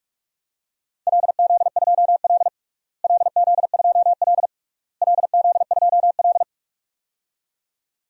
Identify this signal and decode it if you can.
Morse code, heard as a keyed tone, about 44 wpm, 690 Hz. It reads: LZ2L LZ2L LZ2L
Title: DXCC-100 Call Signs - 4 Characters - Repeated 3x - 45wpm